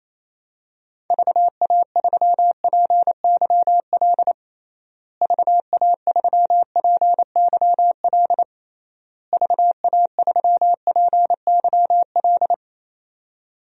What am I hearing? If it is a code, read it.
Morse code, heard as a keyed tone, about 28 wpm, 700 Hz. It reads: VA3PYL VA3PYL VA3PYL
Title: DXCC-100 Call Signs - 6 Characters - Repeated 3x - 28wpm